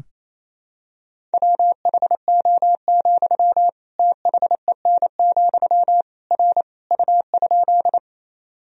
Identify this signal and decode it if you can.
Morse code, heard as a keyed tone, about 28 wpm, 700 Hz. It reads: WHO, THEN, R U?
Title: Sentences from Top 100 Words and Abbreviations 28wpm